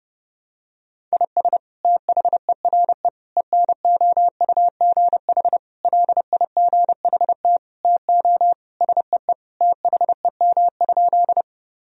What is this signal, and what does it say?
Morse code, heard as a keyed tone, about 30 wpm, 700 Hz. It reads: IS THERE ENOUGH LIGHT TO SEE THEM?
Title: Sentences from Top 1000 Words - Review Part 1 - 30wpm